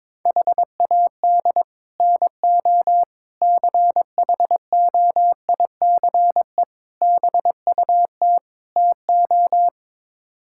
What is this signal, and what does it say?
Morse code, heard as a keyed tone, about 21 wpm, 700 Hz. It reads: HAD NO CHOICE BUT TO
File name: practice / 22wpm / Build A Phrase - 5-Word Phrases - 22wpm